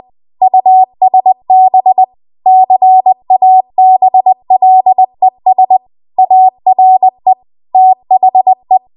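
Morse code, heard as a keyed tone, about 20 wpm, 750 Hz. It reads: USB CABLES ARE THE
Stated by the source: Morse code practice at 20 WPM.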